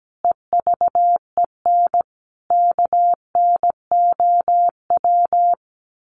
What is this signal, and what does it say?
Morse code, heard as a keyed tone, about 17 wpm, 700 Hz. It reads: EVEN KNOW
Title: Sets of 4 Words 17wpm